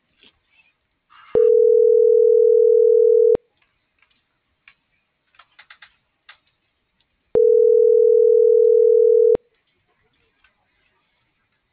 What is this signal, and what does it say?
Ringback tone (North America), 440 + 480 Hz.